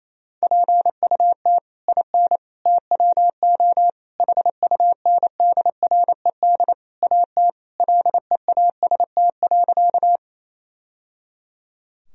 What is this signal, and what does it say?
Morse code, heard as a keyed tone, about 28 wpm, 700 Hz. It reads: PUT IN TWO HUNDRED AT LEAST.
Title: Sentences from Top 1000 Words - Review Part 2 - 28wpm